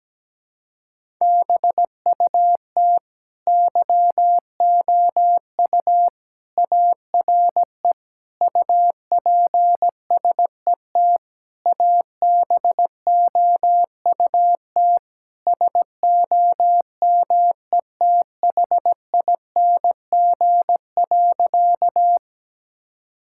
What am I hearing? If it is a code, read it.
Morse code, heard as a keyed tone, about 17 wpm, 700 Hz. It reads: BUT YOU ARE UPSET ABOUT SOMETHING.
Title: Sentences from Top 1000 Words - Review Part 5 - 17wpm